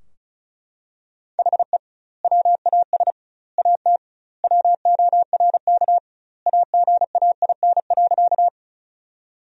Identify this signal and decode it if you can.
Morse code, heard as a keyed tone, about 36 wpm, 700 Hz. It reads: HE WAS AT WORK AGAIN.
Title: 2x Word-Spacing - Sentences from Top 200 Words - 35wpm